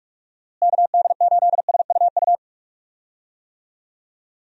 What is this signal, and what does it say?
Morse code, heard as a keyed tone, about 41 wpm, 700 Hz. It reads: KD8SUU
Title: USA Call Signs - 6 Characters - Repeated 2x - 45wpm